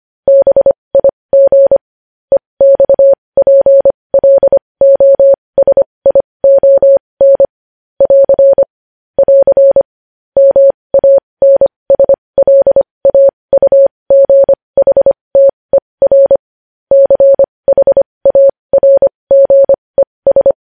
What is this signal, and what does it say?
Morse code, heard as a keyed tone, about 25 wpm, spaced as for 20 wpm, 550 Hz. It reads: BIG EXPLOSION <AR> <AR> MANSLAUGHTER CHARGES
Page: News Headlines in Morse Code at 20 WPM